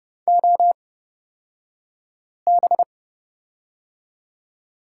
Morse code, heard as a keyed tone, about 30 wpm, spaced as for 5 wpm, 700 Hz. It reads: OB
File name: Lesson 38d - Letter q - Two Characters 30wpm